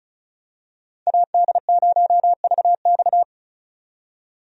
Morse code, heard as a keyed tone, about 36 wpm, 700 Hz. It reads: AD0VX
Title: USA Call Signs - 5 Characters - 35wpm